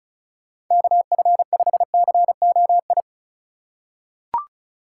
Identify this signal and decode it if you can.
Morse code, heard as a keyed tone, about 34 wpm, 700 Hz. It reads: KF5COI
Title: USA Call Signs - 6 Characters - 4x Character Spacing - 35wpm